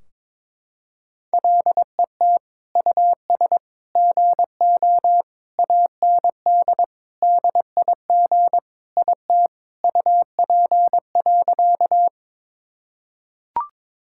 Morse code, heard as a keyed tone, about 21 wpm, 700 Hz. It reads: LET US GO AND DIG IT UP.
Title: Sentences from Top 1600 Words 22wpm